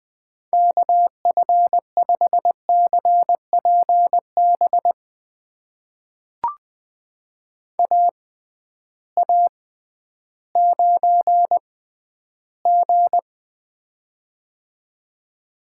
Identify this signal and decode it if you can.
Morse code, heard as a keyed tone, about 20 wpm, spaced as for 12 wpm, 700 Hz. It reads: KF5CPB AA9G
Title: USA Call Signs - 6x Character Spacing - 20wpm